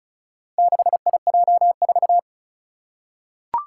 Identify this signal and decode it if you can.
Morse code, heard as a keyed tone, about 36 wpm, 700 Hz. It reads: 6IJ4
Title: Four Challenging Letter-Number Combinations 35wpm